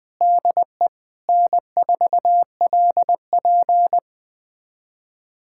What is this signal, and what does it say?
Morse code, heard as a keyed tone, about 21 wpm, 700 Hz. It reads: DE N4LP
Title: USA Callsign DE Callsign - 4 Characters - 20wpm